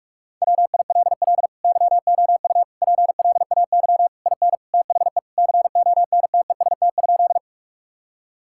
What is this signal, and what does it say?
Morse code, heard as a keyed tone, about 46 wpm, 700 Hz. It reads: WILL YOU PLAY IN THE CONTEST?
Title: Sentences from Top 1600 Words 45wpm